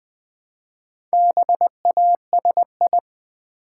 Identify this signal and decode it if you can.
Morse code, heard as a keyed tone, about 20 wpm, 700 Hz. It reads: BASI
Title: Build A Word - 1 Letter At A Time - 3 to 5-Letter Words - 20wpm